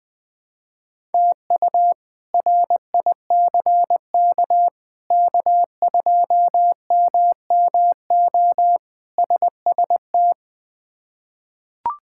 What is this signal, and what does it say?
Morse code, heard as a keyed tone, about 21 wpm, 700 Hz. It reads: TU RICK K2MMO SST
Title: K1USN SST QSO Practice - 20wpm